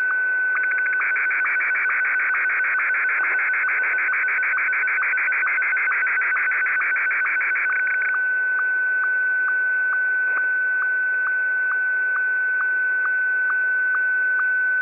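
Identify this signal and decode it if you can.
SSTV (slow-scan television), Martin 1.
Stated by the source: Received on a WebSDR at 8190.0 kHz.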